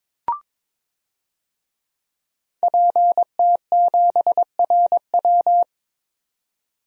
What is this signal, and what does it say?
Morse code, heard as a keyed tone, about 22 wpm, 700 Hz. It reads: PT7RW
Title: DXCC-100 Call Signs - 5 Characters - 22wpm